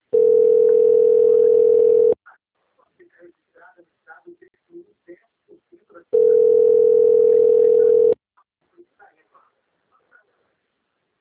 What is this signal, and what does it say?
Ringback tone (North America), 440 + 480 Hz, with speech.
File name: out- - - - - .5068